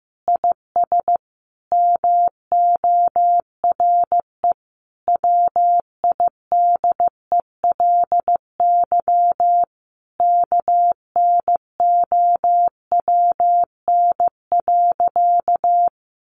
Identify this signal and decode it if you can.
Morse code, heard as a keyed tone, about 15 wpm, 700 Hz. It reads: IS MORE WIDELY KNOWN.
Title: Sentences from Top 1200 Words 15wpm